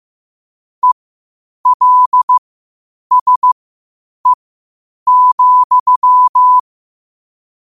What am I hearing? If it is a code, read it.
Morse code, heard as a keyed tone, about 15 wpm, spaced as for 8 wpm, 1000 Hz. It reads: ELSE,
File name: qotdfiles / 8wpm